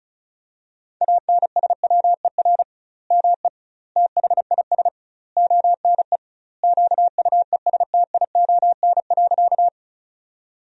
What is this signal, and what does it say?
Morse code, heard as a keyed tone, about 36 wpm, 700 Hz. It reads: ANSWER ME THIS ONE QUESTION.
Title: 2x Word-Spacing - Sentences from Top 400 Words - 35wpm